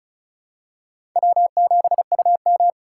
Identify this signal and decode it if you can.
Morse code, heard as a keyed tone, about 34 wpm, 700 Hz. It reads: W7UM
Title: USA Call Signs - 4 Characters - 3x Character Spacing - 35wpm